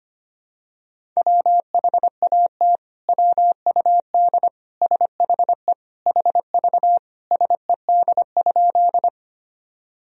Morse code, heard as a keyed tone, about 25 wpm, 700 Hz. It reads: WHAT WUD SHE HV SED?
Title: Sentences from Top 100 Words and Abbreviations 25wpm